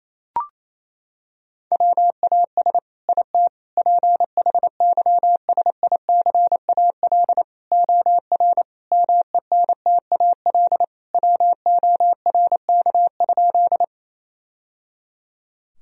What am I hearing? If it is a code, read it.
Morse code, heard as a keyed tone, about 28 wpm, 700 Hz. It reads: WAS IT PHYSICAL OR MENTAL WORK?
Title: Sentences from Top 1000 Words - Review Part 4 - 28wpm